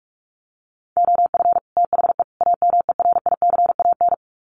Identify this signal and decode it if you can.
Morse code, heard as a keyed tone, about 44 wpm, 700 Hz. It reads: OF THE AMERICAN
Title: Sets of 3 Words 45wpm